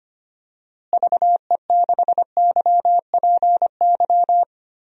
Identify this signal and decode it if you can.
Morse code, heard as a keyed tone, about 25 wpm, 700 Hz. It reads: VE6YPY
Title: DXCC-100 Call Signs - 6 Characters - Repeated 2x - 25wpm